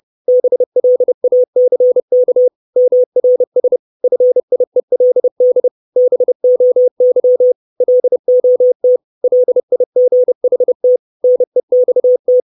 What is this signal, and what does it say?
Morse code, heard as a keyed tone, about 30 wpm, 500 Hz. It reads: BLACK MRS FIELD BOY LOT LIGHT NEXT
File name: 632 Words 30WPM